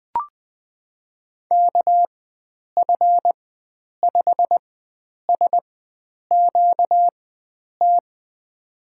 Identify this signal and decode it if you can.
Morse code, heard as a keyed tone, about 20 wpm, spaced as for 9 wpm, 700 Hz. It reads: KF5SQT
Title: USA Call Signs - 6 Characters - 4x Character Spacing - 20wpm